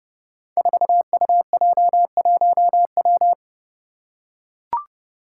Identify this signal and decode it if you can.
Morse code, heard as a keyed tone, about 30 wpm, 700 Hz. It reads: 4UJ1W
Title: Five Challenging Letter-Number Combinations 30wpm